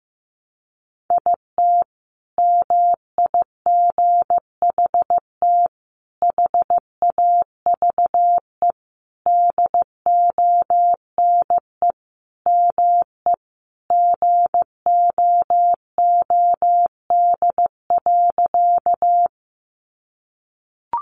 Morse code, heard as a keyed tone, about 15 wpm, 700 Hz. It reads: IT MIGHT HAVE DONE ME GOOD.
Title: Sentences from Top 500 Words 15wpm